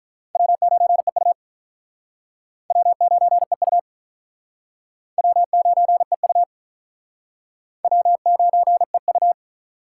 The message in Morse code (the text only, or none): W9EU W9EU W9EU W9EU